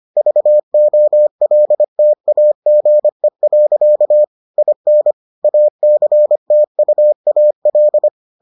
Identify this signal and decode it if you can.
Morse code, heard as a keyed tone, about 25 wpm, 600 Hz. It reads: VOLTAGE. IN ACTUAL